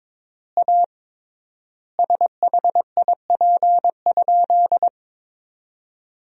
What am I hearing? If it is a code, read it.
Morse code, heard as a keyed tone, about 22 wpm, 700 Hz. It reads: A SHIP?
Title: 3x Word-Spacing - Sentences from Top 400 Words - 22wpm